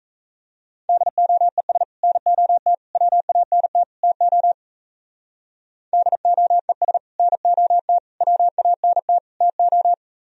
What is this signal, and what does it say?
Morse code, heard as a keyed tone, about 35 wpm, 700 Hz. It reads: DOES NOT WANT TO DOES NOT WANT TO
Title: Speed-Racing - Sets of 4 Words 28wpm